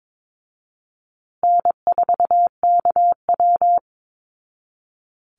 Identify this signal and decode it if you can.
Morse code, heard as a keyed tone, about 22 wpm, 700 Hz. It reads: N4KW
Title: Easy Call Signs Repeated 3x - 22wpm